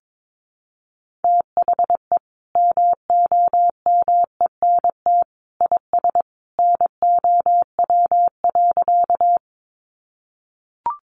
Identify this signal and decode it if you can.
Morse code, heard as a keyed tone, about 22 wpm, 700 Hz. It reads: THE MOMENT IS NOW.
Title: Sentences from Top 1000 Words 22wpm